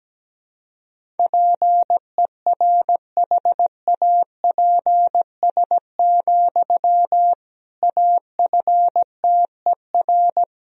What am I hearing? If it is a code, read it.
Morse code, heard as a keyed tone, about 17 wpm, 700 Hz. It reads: PERHAPS, AFTER
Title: Sentences from Top 500 Words and Abbreviations 17wpm